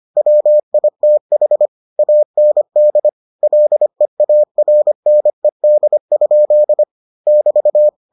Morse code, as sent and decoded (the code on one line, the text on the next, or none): .-- .. - .... / .- -. -.. / .-.. .- .-. -. -.. ..--.. / -...-
WITH AND LEARNED? <BT>